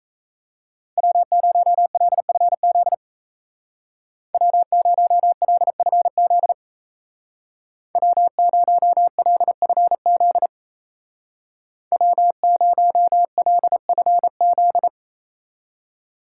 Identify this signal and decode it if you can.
Morse code, heard as a keyed tone, about 33 wpm, 700 Hz. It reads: W0LFZ W0LFZ W0LFZ W0LFZ
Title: Speed Racing - USA 1x3 Call Signs 28wpm